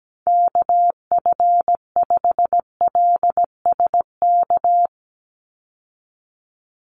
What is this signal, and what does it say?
Morse code, heard as a keyed tone, about 17 wpm, 700 Hz. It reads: KF5LSK
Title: USA Call Signs 17wpm